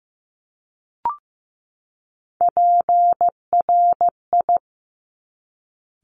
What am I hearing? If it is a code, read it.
Morse code, heard as a keyed tone, about 15 wpm, 700 Hz. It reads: PRI
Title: Top Three Letter Combinations in Words 15wpm